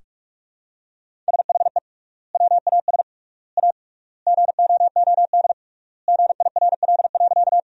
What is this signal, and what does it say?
Morse code, heard as a keyed tone, about 45 wpm, 700 Hz. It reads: SHE WAS A GOOD GIRL.
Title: 3x Word-Spacing - Sentences from Top 400 Words - 45wpm